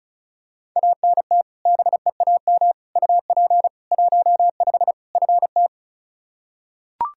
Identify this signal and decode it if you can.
Morse code, heard as a keyed tone, about 36 wpm, 700 Hz. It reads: ANT BEAM UP 15 FT
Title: Practice QSO - Exchange 2 - Info - 35wpm